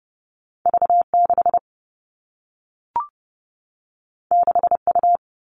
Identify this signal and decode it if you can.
Morse code, heard as a keyed tone, about 30 wpm, 700 Hz. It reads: V6 6U
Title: All Cut Numbers - 2-Digit 30wpm